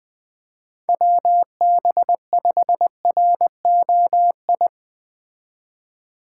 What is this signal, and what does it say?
Morse code, heard as a keyed tone, about 21 wpm, 700 Hz. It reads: WB5ROI